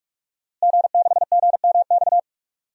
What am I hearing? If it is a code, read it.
Morse code, heard as a keyed tone, about 47 wpm, 700 Hz. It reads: G6GMX
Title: DXCC-100 Call Signs - 5 Characters - Repeated 3x - 45wpm